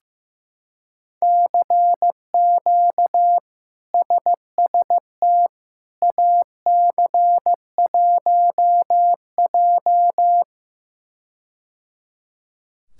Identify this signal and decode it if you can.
Morse code, heard as a keyed tone, about 15 wpm, 700 Hz. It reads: CQ SST AC1J
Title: K1USN SST QSO Practice - 15wpm